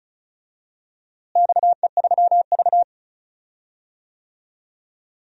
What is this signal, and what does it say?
Morse code, heard as a keyed tone, about 36 wpm, 700 Hz. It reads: XE3V
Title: DXCC-100 Call Signs - 4 Characters - Repeated 2x - 35wpm